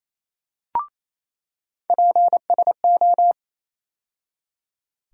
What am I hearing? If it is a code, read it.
Morse code, heard as a keyed tone, about 28 wpm, 700 Hz. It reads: PSO
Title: USA Call Sign Postfixes - 3 Letters 28wpm